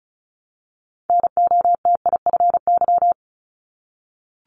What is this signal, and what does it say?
Morse code, heard as a keyed tone, about 36 wpm, 700 Hz. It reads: NOTIFY
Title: Multi-letter Phonemes Repeated 3x 35wpm